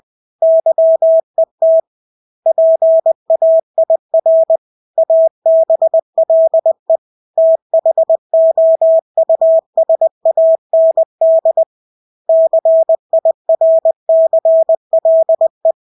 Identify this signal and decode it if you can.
Morse code, heard as a keyed tone, about 20 wpm, 650 Hz. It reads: YET PAIR ABLE THOUSAND CIRCLE
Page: Engelske ord 20 wpm